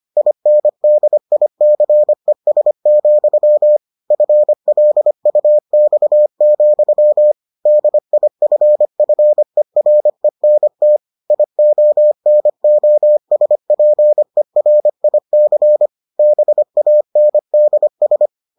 Morse code, heard as a keyed tone, about 25 wpm, 600 Hz. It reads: INDICES, FLUX, DIFFERENT IONOSPERIC BANDS